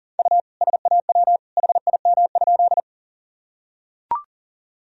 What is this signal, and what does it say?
Morse code, heard as a keyed tone, about 40 wpm, 700 Hz. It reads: U SAW HIM?
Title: Sentences from Top 300 Words and Abbreviations 40wpm